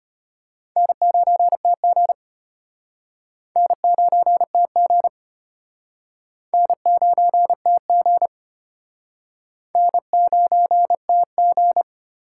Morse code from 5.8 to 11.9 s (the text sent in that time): N9TG N9TG